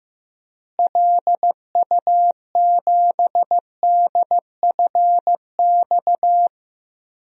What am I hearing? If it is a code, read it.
Morse code, heard as a keyed tone, about 15 wpm, 700 Hz. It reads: LU7DFX
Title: DXCC-100 Call Signs - 6 Characters - Repeated 2x - 15wpm